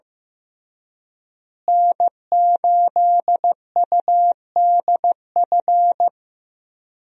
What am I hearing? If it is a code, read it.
Morse code, heard as a keyed tone, about 15 wpm, 700 Hz. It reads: N8UDF
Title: USA Call Signs - 5 Characters - Repeated 2x - 15wpm